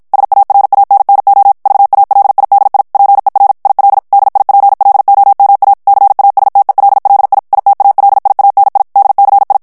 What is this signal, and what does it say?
Morse code, heard as a keyed tone, about 50 wpm, 800 Hz. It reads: SARANNO VALIDI PER IL DIPLOMA CASTELLI ITALIANI DCI
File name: morse mp3 / ITALIEN texte clair 1 vitesse 50 mots minute